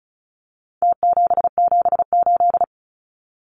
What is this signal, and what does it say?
Morse code, heard as a keyed tone, about 34 wpm, 700 Hz. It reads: T778
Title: Numbers - Contest Style - 4-Digit 35wpm